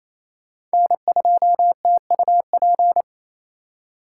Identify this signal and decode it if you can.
Morse code, heard as a keyed tone, about 36 wpm, 700 Hz. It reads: N2TUP